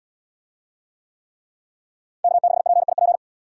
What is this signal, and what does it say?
Morse code, heard as a keyed tone, about 75 wpm, 700 Hz. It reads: K6PIP